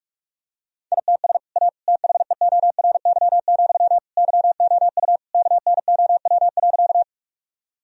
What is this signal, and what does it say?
Morse code, heard as a keyed tone, about 46 wpm, 700 Hz. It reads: ITS A THEORY, YOU KNOW.
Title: Sentences from Top 1000 Words - Review Part 3 - 45wpm